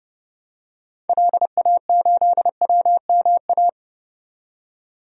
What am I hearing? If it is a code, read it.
Morse code, heard as a keyed tone, about 30 wpm, 700 Hz. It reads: LA8WMA
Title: DXCC-100 Call Signs - 6 Characters - 30wpm